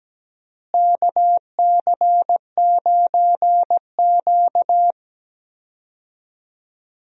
Morse code, heard as a keyed tone, about 17 wpm, 700 Hz. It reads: KC9Q
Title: USA Call Signs - 4 Characters - Repeated 3x - 17wpm